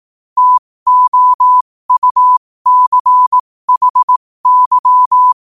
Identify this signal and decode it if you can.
Morse code, heard as a keyed tone, about 18 wpm, spaced as for 15 wpm, 1000 Hz. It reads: TOUCHY
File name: qotdfiles / 15wpm